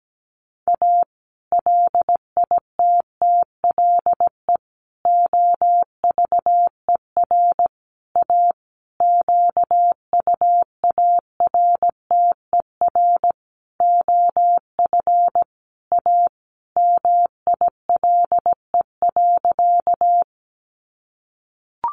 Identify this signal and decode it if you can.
Morse code, heard as a keyed tone, about 17 wpm, 700 Hz. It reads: A LITTLE OVER A QUARTER OF A MILE.
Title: Sentences from Top 1300 Words 17wpm